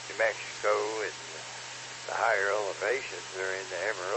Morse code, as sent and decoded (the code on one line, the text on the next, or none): none